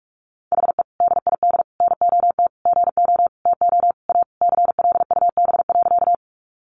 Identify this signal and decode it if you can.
Morse code, heard as a keyed tone, about 44 wpm, 700 Hz. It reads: HE DID NOT GO TO A CLUB.
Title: Sentences from Top 1200 Words 45wpm